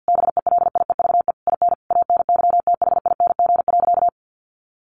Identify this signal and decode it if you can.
Morse code, heard as a keyed tone, about 50 wpm, 700 Hz. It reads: BELIEVE IN ANYTHING.
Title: Sentences from Top 1500 Words 50wpm